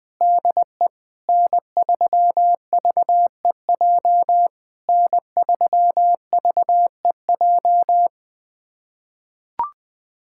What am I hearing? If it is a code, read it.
Morse code, heard as a keyed tone, about 20 wpm, 700 Hz. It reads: DE N3VEJ N3VEJ
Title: POTA QSO Practice - 20wpm